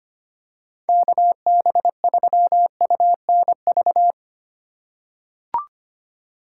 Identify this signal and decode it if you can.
Morse code, heard as a keyed tone, about 25 wpm, 700 Hz. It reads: KB3UNV
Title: USA Call Signs - 6 Characters - 2x Character Spacing - 25wpm